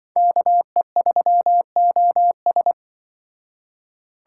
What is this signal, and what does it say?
Morse code, heard as a keyed tone, about 24 wpm, 700 Hz. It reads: KE3OS
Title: Speed Racing - USA 2x2 Call Signs 20wpm